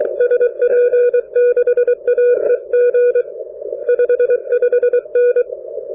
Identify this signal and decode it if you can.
Morse code, heard as a keyed tone, about 23 wpm, 2450 Hz, with noise. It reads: SP6AEG 55N
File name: n6tr / N6TR at TF4M 10-Jan-2010